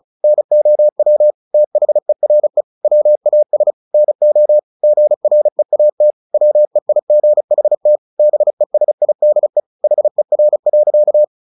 Morse code, heard as a keyed tone, about 34 wpm, 600 Hz. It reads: NOW THERE WAS NO GREAT WEIGHT BESIDE HER.
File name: the old man and the sea 2 35wpm